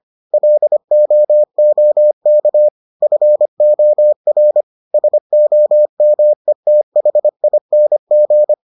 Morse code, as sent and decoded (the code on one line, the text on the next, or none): .-.. --- --- -.- / ..-. --- .-. / ... --- -- . - .... .. -. --.
LOOK FOR SOMETHING